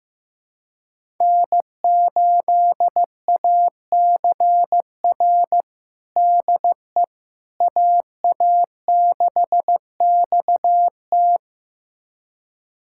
Morse code, heard as a keyed tone, about 16 wpm, 700 Hz. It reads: N8ACR DE AA6XT
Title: USA Callsign DE Callsign 15wpm